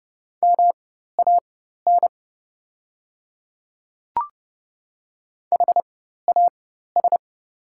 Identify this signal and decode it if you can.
Morse code, heard as a keyed tone, about 29 wpm, spaced as for 14 wpm, 700 Hz. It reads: MAN HAS
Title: 4x Character Spacing - Top 500 Words - 30wpm